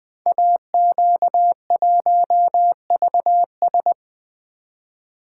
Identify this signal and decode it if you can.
Morse code, heard as a keyed tone, about 20 wpm, 700 Hz. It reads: AQ1VS
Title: Code Groups - 5 Letter:Numbers - 20wpm